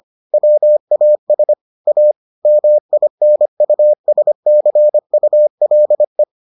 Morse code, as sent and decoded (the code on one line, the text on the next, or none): .-- .- ... / .- / -- .. -. ..- ... -.-. ..- .-.. .
WAS A MINUSCULE